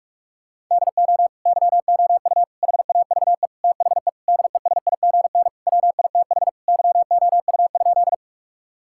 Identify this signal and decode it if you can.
Morse code, heard as a keyed tone, about 46 wpm, 700 Hz. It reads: DO YOU HAVE THE DESIGN WITH YOU?
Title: Sentences from Top 1000 Words - Review Part 5 - 45wpm